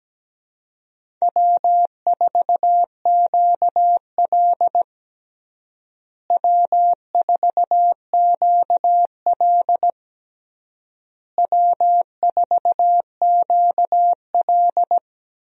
Morse code, heard as a keyed tone, about 17 wpm, 700 Hz. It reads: W4QL W4QL W4QL
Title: USA Call Signs - 4 Characters - Repeated 3x - 17wpm